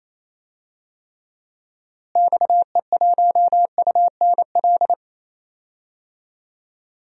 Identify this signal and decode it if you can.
Morse code, heard as a keyed tone, about 28 wpm, 700 Hz. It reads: XE1UNL
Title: DXCC-100 Call Signs - 6 Characters - Repeated 3x - 28wpm